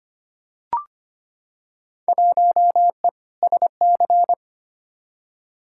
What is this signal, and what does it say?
Morse code, heard as a keyed tone, about 25 wpm, 700 Hz. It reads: 1E SC
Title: Field Day Exchanges - 25wpm